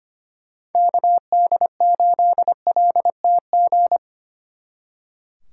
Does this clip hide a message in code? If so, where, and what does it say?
Yes: Morse code from 0.8 to 4.0 s, heard as a keyed tone, about 31 wpm, 700 Hz. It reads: KD8LTG